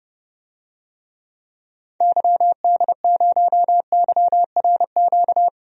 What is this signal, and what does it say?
Morse code, heard as a keyed tone, about 30 wpm, 700 Hz. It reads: YD0YRQ
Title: DXCC-100 Call Signs - 6 Characters - Repeated 3x - 30wpm